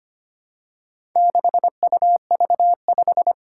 Morse code, heard as a keyed tone, about 25 wpm, 700 Hz. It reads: 6UV5
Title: Four Challenging Letter-Number Combinations 25wpm